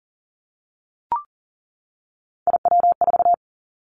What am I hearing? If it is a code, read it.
Morse code, heard as a keyed tone, about 40 wpm, 700 Hz. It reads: IW4
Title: Callsign Prefixes for Top 25 DXCC - 40wpm